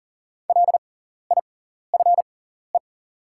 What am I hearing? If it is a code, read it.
Morse code, heard as a keyed tone, about 40 wpm, spaced as for 14 wpm, 700 Hz. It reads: LIFE
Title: 6x Character Spacing - Top 300 Words - 40wpm